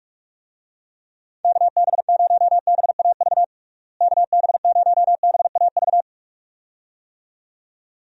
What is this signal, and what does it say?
Morse code, heard as a keyed tone, about 46 wpm, 700 Hz. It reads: KB0BAV KB0BAV
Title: USA Call Signs - 6 Characters - Repeated 2x - 45wpm